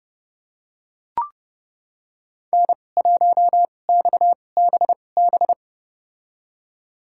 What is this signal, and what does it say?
Morse code, heard as a keyed tone, about 30 wpm, spaced as for 22 wpm, 700 Hz. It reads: N1XBB
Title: USA Call Signs - 5 Characters - 2x Character Spacing - 30wpm